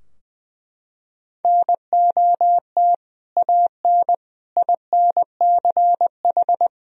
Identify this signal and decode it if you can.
Morse code, heard as a keyed tone, about 20 wpm, 700 Hz. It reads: NOT AN INCH
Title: Sentences from Top 1000 Words - Review Part 3 - 20wpm